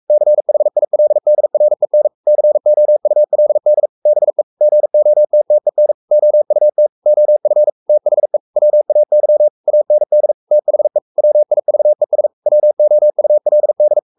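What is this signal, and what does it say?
Morse code, heard as a keyed tone, about 44 wpm, 600 Hz. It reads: CHILDREN COULD BE GOTTEN OUT OF THE WAY AND THE WIVES WOULD
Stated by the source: Morse code at 43 wpm sending Ken Follett's novel Never in English, from CW med Gnister.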